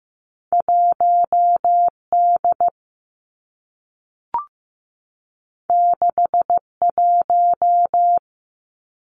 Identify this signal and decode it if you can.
Morse code, heard as a keyed tone, about 15 wpm, 700 Hz. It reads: 1 D 6 1
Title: Two Challenging Letter-Number Combinations 15wpm